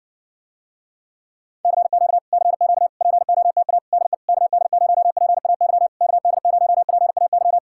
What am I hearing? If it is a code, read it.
Morse code, heard as a keyed tone, about 60 wpm, 700 Hz. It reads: CQ CQ POTA DE KD0PAY KD0PAY